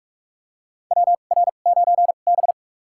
Morse code, heard as a keyed tone, about 46 wpm, spaced as for 32 wpm, 700 Hz. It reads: WR9B
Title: USA Call Signs - 2x Character Spacing - 45wpm